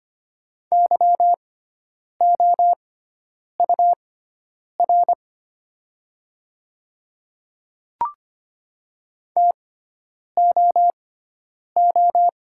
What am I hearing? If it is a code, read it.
Morse code, heard as a keyed tone, about 25 wpm, spaced as for 9 wpm, 690 Hz. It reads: YOUR TOO
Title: 6x Character Spacing - Top 200 Words - 25wpm